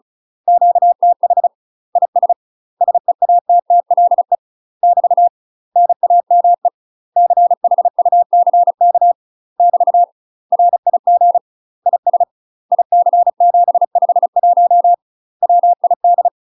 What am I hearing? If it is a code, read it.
Morse code, heard as a keyed tone, about 36 wpm, 700 Hz. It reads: QTH IS SEATTLE <BT> NAME CHUCK <BT> RIG IS IC751 WID
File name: UK MASTER QSOs TRAINING / 35WPM / UK MASTER QSOs 0035